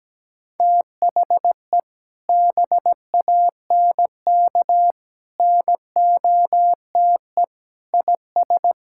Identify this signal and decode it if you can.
Morse code, heard as a keyed tone, about 17 wpm, 700 Hz. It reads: THE BANK NOTE IS
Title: Sentences from Top 1000 Words - Review Part 6 - 17wpm